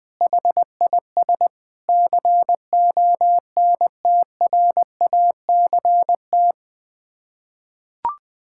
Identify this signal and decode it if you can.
Morse code, heard as a keyed tone, about 20 wpm, 700 Hz. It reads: HIS CONTRACT
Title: Sets of 8 Words - 20wpm